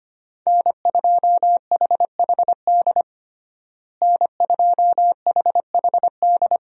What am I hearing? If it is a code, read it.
Morse code, heard as a keyed tone, about 25 wpm, 700 Hz. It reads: N2HHD N2HHD
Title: USA Call Signs - 5 Characters - Repeated 2x - 25wpm